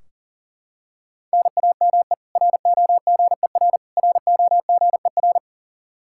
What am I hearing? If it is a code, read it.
Morse code, heard as a keyed tone, about 40 wpm, 700 Hz. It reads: NAME ROGER ROGER
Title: Practice QSO - Exchange 1 - Beginning - 40wpm